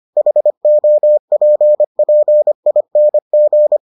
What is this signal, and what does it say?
Morse code, heard as a keyed tone, about 25 wpm, 600 Hz. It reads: HOPPING